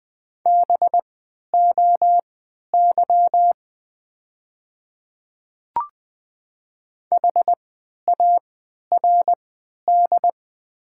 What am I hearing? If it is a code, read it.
Morse code, heard as a keyed tone, about 20 wpm, spaced as for 11 wpm, 700 Hz. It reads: BOY HARD
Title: 3x Character Spacing - Top 300 Words - 20wpm